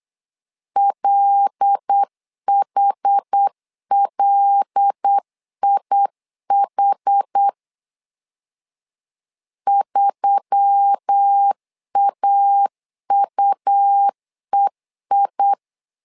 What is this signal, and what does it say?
Morse code, heard as a keyed tone, about 8 wpm, 800 Hz. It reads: LHLIH 3AUEI
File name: LEZIONI / 5 LBVC / 5 L 8wpm 2831